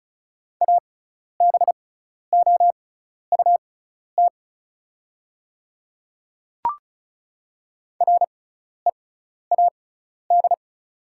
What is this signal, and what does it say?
Morse code, heard as a keyed tone, about 36 wpm, spaced as for 12 wpm, 700 Hz. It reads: ABOUT READ
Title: 6x Character Spacing - Top 200 Words - 35wpm